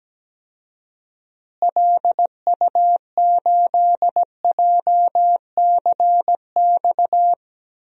Morse code, heard as a keyed tone, about 17 wpm, 700 Hz. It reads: LU8JCX
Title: DXCC-100 Call Signs - 6 Characters - Repeated 3x - 17wpm